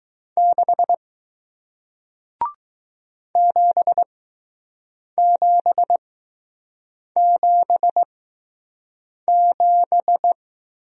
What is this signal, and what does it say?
Morse code, heard as a keyed tone, about 20 wpm, spaced as for 6 wpm, 700 Hz. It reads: 6 7777